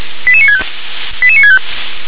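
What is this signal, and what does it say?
Five-tone selective call (ZVEI), heard as quick runs of five tones, tones of about 70 ms. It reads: 88965 88965
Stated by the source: ZVEI five-tone selective call received by SDR.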